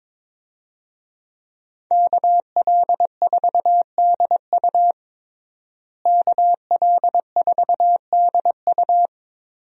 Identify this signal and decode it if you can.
Morse code, heard as a keyed tone, about 22 wpm, 700 Hz. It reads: KL4DU KL4DU
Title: USA Call Signs - 5 Characters - Repeated 3x - 22wpm